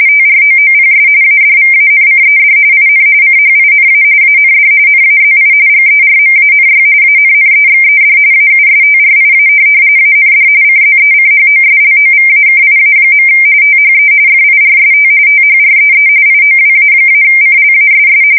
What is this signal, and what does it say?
RTTY, about 45.45 baud. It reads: TROOPS ADVANCED IN CHINA AND NAZI GERMANY ANNEXED AUSTRIA AND CZECHOSLOVAKIA. THE FIRST THREE IOWA-CLASS
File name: fsk170 / USS-New-Jersey